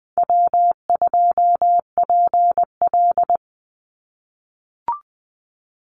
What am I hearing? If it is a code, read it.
Morse code, heard as a keyed tone, about 20 wpm, 700 Hz. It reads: W2PL
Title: Easy Call Signs - 20wpm